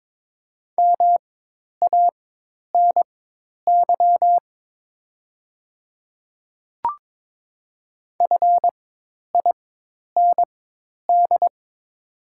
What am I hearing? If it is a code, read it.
Morse code, heard as a keyed tone, about 22 wpm, spaced as for 10 wpm, 700 Hz. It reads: MANY FIND
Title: 4x Character Spacing - Top 100 Words - 22wpm